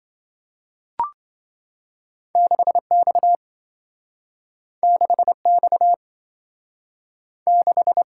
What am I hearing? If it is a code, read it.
Morse code, heard as a keyed tone, about 27 wpm, 700 Hz. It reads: 6X 6X 6